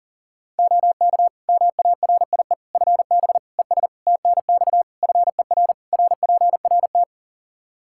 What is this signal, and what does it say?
Morse code, heard as a keyed tone, about 40 wpm, 700 Hz. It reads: OK MARIE FB ES TNX FER RPRT
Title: Practice QSO - Exchange 2 - Info - 40wpm